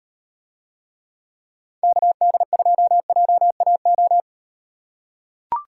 Morse code, heard as a keyed tone, about 38 wpm, 700 Hz. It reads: KD2JAO